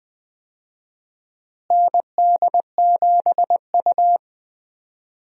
Morse code, heard as a keyed tone, about 20 wpm, 700 Hz. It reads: ND7U
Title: USA Call Signs - 4 Characters - Repeated 3x - 20wpm